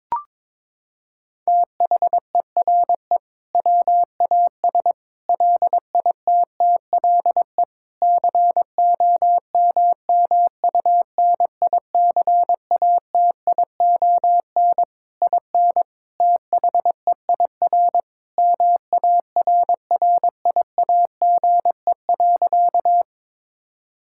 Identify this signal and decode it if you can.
Morse code, heard as a keyed tone, about 22 wpm, 700 Hz. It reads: THERE WAS LITTLE COMMUNICATION IN THEIR MARRIAGE.
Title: Sentences from Top 1000 Words - Review Part 6 - 22wpm